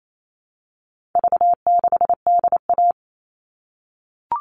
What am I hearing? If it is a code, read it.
Morse code, heard as a keyed tone, about 28 wpm, 700 Hz. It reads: V6DA
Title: All Cut Numbers - 4-Digit 28wpm